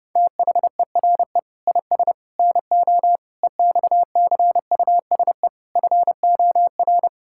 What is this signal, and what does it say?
Morse code, heard as a keyed tone, about 30 wpm, 700 Hz. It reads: THERE IS NO EXCUSE FOR
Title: Sentences from Top 1000 Words - Review Part 6 - 30wpm